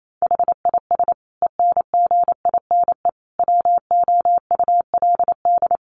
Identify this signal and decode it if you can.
Morse code, heard as a keyed tone, about 28 wpm, 700 Hz. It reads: HIS ENGINE WOULD
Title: Sentences from Top 1500 Words 28wpm